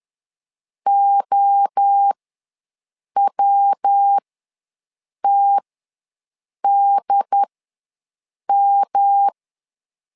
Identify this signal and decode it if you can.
Morse code, heard as a keyed tone, about 10 wpm, spaced as for 6 wpm, 800 Hz. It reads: OWTDM
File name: LEZIONI / 3 MWRO / 3 MWRO ws 10wpm 2310